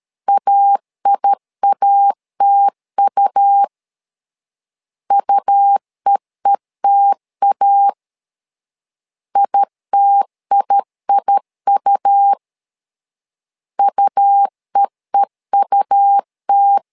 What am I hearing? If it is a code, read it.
Morse code, heard as a keyed tone, about 13 wpm, 800 Hz. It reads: AIATU UEETA ITIIU UEEUT
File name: LEZIONI / 2 UHDN / 2 U 12wpm 1338